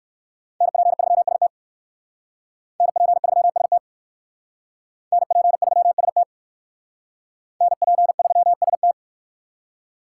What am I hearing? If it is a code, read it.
Morse code, heard as a keyed tone, about 57 wpm, 700 Hz. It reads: NP3ST NP3ST NP3ST NP3ST